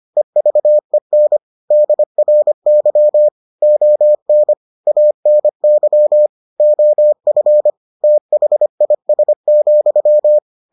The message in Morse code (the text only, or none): EVEN DRY ON ANY OF THIS,